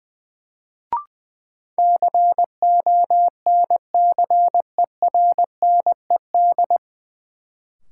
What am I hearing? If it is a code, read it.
Morse code, heard as a keyed tone, about 21 wpm, 700 Hz. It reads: CONCERNED
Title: Sentences from Top 1000 Words - Review Part 6 - New Word Drill - 20wpm